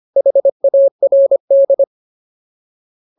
Morse code, heard as a keyed tone, about 25 wpm, 550 Hz. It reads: HARD
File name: Ultimate Word List / 25WPM Ultimate